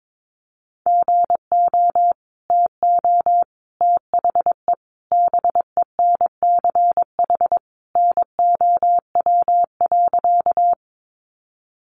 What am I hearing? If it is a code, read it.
Morse code, heard as a keyed tone, about 22 wpm, 700 Hz. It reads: GO TO THE BENCH NOW.
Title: Sentences from Top 1400 Words 22wpm